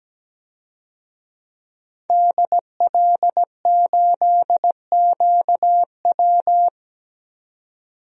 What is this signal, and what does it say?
Morse code, heard as a keyed tone, about 17 wpm, 700 Hz. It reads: DL8QW
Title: DXCC-100 Call Signs - 5 Characters - 17wpm